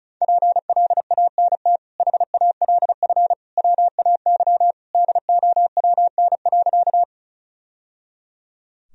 Morse code, heard as a keyed tone, about 36 wpm, 700 Hz. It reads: PLANT HALF WAY DOWN.
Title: Sentences from Top 1000 Words - Review Part 1 - 35wpm